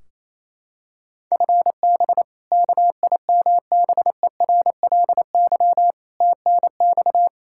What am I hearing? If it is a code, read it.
Morse code, heard as a keyed tone, about 28 wpm, 700 Hz. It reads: FB KIMBERLY TNX
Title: Practice QSO - Exchange 3 - More Info - 28wpm